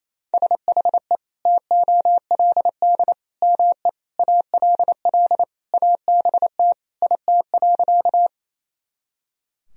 Morse code, heard as a keyed tone, about 28 wpm, 700 Hz. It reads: SHE TOLD ME ALL ABT IT.
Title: Sentences from Top 400 Words and Abbreviations 28wpm